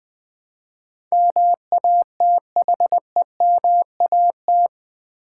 Morse code, heard as a keyed tone, about 20 wpm, 700 Hz. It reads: MATHEMAT
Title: Build A Word - 4 Letters At A Time - 12-Letter Words - 20wpm